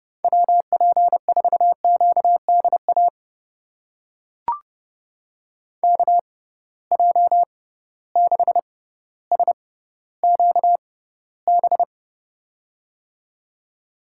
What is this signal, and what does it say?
Morse code, heard as a keyed tone, about 32 wpm, spaced as for 16 wpm, 700 Hz. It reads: WP4QDA KJ6SQB
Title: USA Call Signs - 6 Characters - 6x Character Spacing - 30wpm